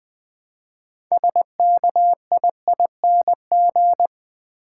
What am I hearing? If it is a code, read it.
Morse code, heard as a keyed tone, about 20 wpm, 700 Hz. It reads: SKIING
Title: Build A Word - 2 Letters At A Time - 6-Letter Words - 20wpm